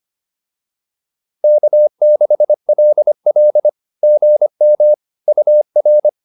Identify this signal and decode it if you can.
Morse code, heard as a keyed tone, about 25 wpm, 600 Hz. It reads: K6LL GM UR
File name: POTA310a 25